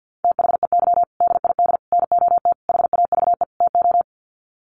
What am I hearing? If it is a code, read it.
Morse code, heard as a keyed tone, about 50 wpm, 700 Hz. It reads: THEY DID NOT HAVE TO
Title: Sets of 5 Words Encore2 50wpm